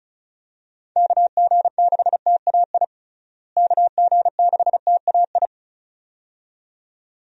Morse code, heard as a keyed tone, about 36 wpm, 700 Hz. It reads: KG6TAI KG6TAI
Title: USA Call Signs - 6 Characters - Repeated 2x - 35wpm